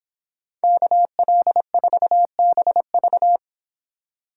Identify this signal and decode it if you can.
Morse code, heard as a keyed tone, about 31 wpm, 700 Hz. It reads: KL4BV